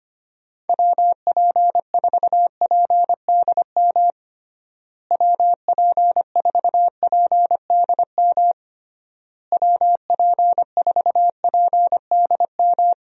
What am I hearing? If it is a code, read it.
Morse code, heard as a keyed tone, about 25 wpm, 700 Hz. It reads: WP4PDM WP4PDM WP4PDM
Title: USA Call Signs - 6 Characters - Repeated 3x - 25wpm